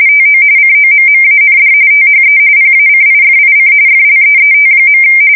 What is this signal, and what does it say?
RTTY, about 45.45 baud. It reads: TO COMMAND THE PACIFIC FLEET.
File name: fsk170 / USS-New-Jersey